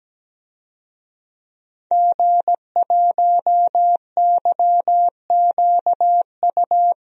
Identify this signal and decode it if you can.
Morse code, heard as a keyed tone, about 17 wpm, 700 Hz. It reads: G1YQU
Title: DXCC-100 Call Signs - 5 Characters - Repeated 2x - 17wpm